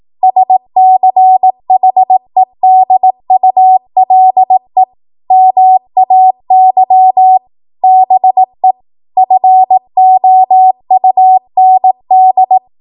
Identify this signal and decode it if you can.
Morse code, heard as a keyed tone, about 18 wpm, 750 Hz. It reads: SCHEDULE MAY BE FOUND
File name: Archive / 18 WPM / 18WPM